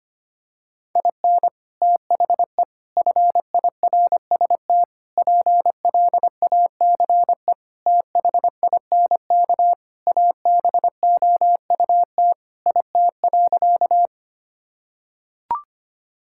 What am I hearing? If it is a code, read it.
Morse code, heard as a keyed tone, about 25 wpm, 700 Hz. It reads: IN THE FIRST PLACE THINK ABOUT IT.
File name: practice / 25wpm / Sentences with 3-Word Phrases - 25wpm